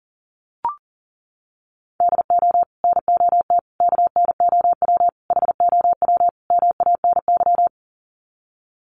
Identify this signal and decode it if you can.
Morse code, heard as a keyed tone, about 40 wpm, 700 Hz. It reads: DO NOT KNOW HOW MANY
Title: Sets of 5 Words Encore 40wpm